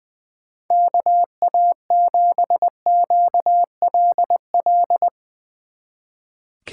Morse code, heard as a keyed tone, about 20 wpm, 700 Hz. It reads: KA7QLL
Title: Speed Racing - USA 2x3 Call Signs 20wpm